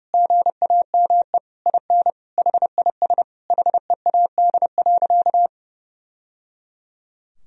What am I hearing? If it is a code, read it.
Morse code, heard as a keyed tone, about 30 wpm, 700 Hz. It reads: GAME IN HIS HEAD.
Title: Sentences from Top 1000 Words - Review Part 3 - 30wpm